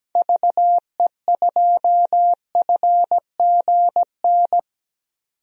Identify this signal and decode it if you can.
Morse code, heard as a keyed tone, about 17 wpm, 700 Hz. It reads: VE2FGN
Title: DXCC-100 Call Signs - 6 Characters - 17wpm